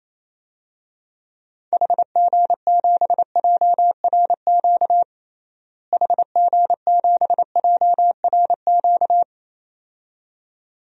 Morse code, heard as a keyed tone, about 27 wpm, 700 Hz. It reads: HG7JRQ HG7JRQ
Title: DXCC-100 Call Signs - 6 Characters - Repeated 2x - 28wpm